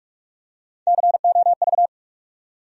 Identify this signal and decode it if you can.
Morse code, heard as a keyed tone, about 46 wpm, 700 Hz. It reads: COV
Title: USA Call Sign Postfixes - 3 Letters 45wpm